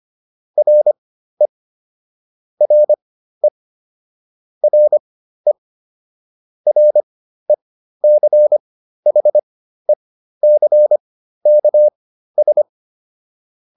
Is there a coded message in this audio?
Yes: Morse code from 0.6 to 12.6 s, heard as a keyed tone, about 25 wpm, spaced as for 13 wpm, 600 Hz. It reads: RE RE RE RECHECKS